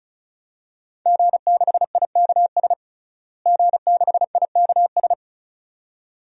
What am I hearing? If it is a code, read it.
Morse code, heard as a keyed tone, about 36 wpm, 700 Hz. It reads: G6IKS G6IKS
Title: DXCC-100 Call Signs - 5 Characters - Repeated 2x - 35wpm